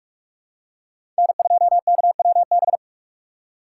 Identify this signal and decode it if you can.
Morse code, heard as a keyed tone, about 43 wpm, 700 Hz. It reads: N2KWB
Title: USA Call Signs - 6x Character Spacing - 45wpm